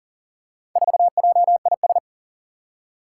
Morse code, heard as a keyed tone, about 37 wpm, 700 Hz. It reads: 4JIS
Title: Four Challenging Letter-Number Combinations 40wpm